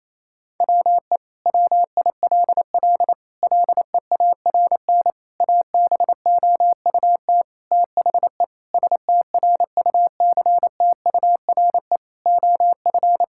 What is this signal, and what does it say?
Morse code, heard as a keyed tone, about 28 wpm, 700 Hz. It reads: WE WILL LEARN ABOUT THE STRUCTURE OF
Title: Sentences from Top 1000 Words - Review Part 6 - 28wpm